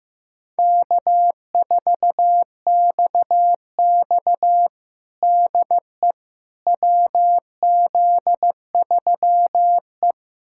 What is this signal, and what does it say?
Morse code, heard as a keyed tone, about 15 wpm, 700 Hz. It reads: K4XX DE WZ3E
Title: USA Callsign DE Callsign - 4 Characters - 15wpm